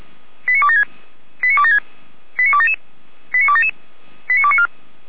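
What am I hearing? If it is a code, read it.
Five-tone selective call (ZVEI), heard as quick runs of five tones, tones of about 70 ms. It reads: 89276 89276 89277 89277 89294
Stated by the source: ZVEI five-tone selective call 89265 received by SDR.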